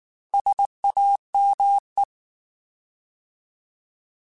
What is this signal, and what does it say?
Morse code, heard as a keyed tone, about 19 wpm, 800 Hz. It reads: SAME
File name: Alphabetical / Alphabetical mp3s / Ss @18